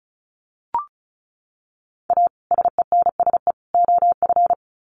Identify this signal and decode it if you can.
Morse code, heard as a keyed tone, about 34 wpm, 700 Hz. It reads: A SENSE OF
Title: Sets of 3 Words 35wpm